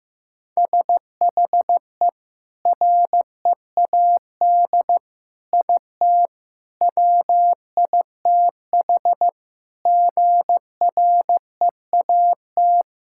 Morse code, heard as a keyed tone, about 15 wpm, 700 Hz. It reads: SHE READ IT WITH GREAT
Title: Sentences from Top 1000 Words - Review Part 6 - 15wpm